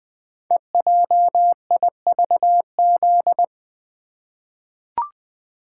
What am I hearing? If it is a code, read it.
Morse code, heard as a keyed tone, about 21 wpm, 700 Hz. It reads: EJIVZ
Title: Code Groups - 5 Letters - 20wpm